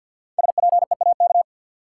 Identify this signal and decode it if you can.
Morse code, heard as a keyed tone, about 52 wpm, 700 Hz. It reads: SPEAK
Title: Sentences from Top 1100 Words - New Word Drill Alt - 50wpm